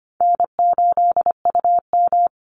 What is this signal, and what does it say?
Morse code, heard as a keyed tone, about 25 wpm, 700 Hz. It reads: N8UM
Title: Easy Call Signs - 25wpm